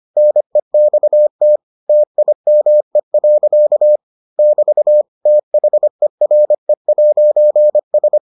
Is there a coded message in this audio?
Yes: Morse code from 0.2 to 8.2 s, heard as a keyed tone, about 25 wpm, 600 Hz. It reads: NEXT TIME. <BT> THERE'S